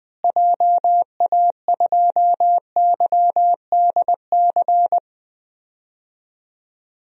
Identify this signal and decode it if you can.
Morse code, heard as a keyed tone, about 20 wpm, 700 Hz. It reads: JA2YDC
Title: DXCC-100 Call Signs - 6 Characters - Repeated 3x - 20wpm